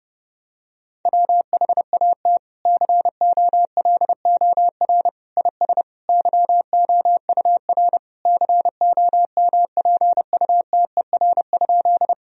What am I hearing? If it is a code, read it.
Morse code, heard as a keyed tone, about 32 wpm, 700 Hz. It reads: WHAT COLOR IS YOUR COMPUTER?
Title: Sentences from Top 1000 Words - Review Part 3 - 30wpm